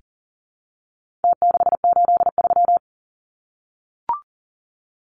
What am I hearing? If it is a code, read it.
Morse code, heard as a keyed tone, about 40 wpm, 700 Hz. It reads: T683
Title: Numbers - Contest Style - 4-Digit 40wpm